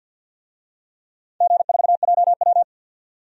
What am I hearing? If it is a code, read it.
Morse code, heard as a keyed tone, about 49 wpm, 700 Hz. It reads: G4JW
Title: DXCC-100 Call Signs - 4 Characters - 50wpm